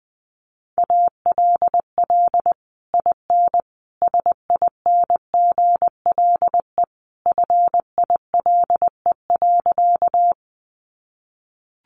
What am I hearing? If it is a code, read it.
Morse code, heard as a keyed tone, about 20 wpm, 700 Hz. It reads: ALL IN SINGLE FILE.
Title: Sentences from Top 1200 Words 20wpm